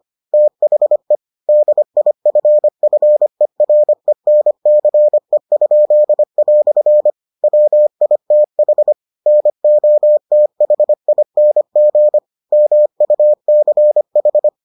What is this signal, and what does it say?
Morse code, heard as a keyed tone, about 25 wpm, 600 Hz. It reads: THE DIFFERENCE?" WITH NOTHING MUCH